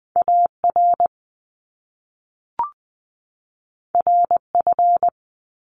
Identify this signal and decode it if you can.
Morse code, heard as a keyed tone, about 20 wpm, 700 Hz. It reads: AR RF
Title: Top Two Letter Combinations in Words 20wpm